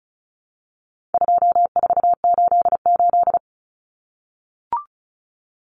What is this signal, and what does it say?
Morse code, heard as a keyed tone, about 34 wpm, 700 Hz. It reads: 2488
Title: Common Cut Numbers - 4-Digit 35wpm